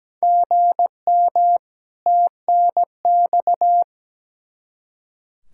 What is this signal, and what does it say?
Morse code, heard as a keyed tone, about 17 wpm, 700 Hz. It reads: GM TNX
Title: Practice QSO - Exchange 1 - Beginning - 17wpm